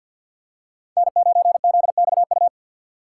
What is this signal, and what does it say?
Morse code, heard as a keyed tone, about 54 wpm, 700 Hz. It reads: N9ZXU
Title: USA Call Signs - 5 Characters - 8x Character Spacing - 50wpm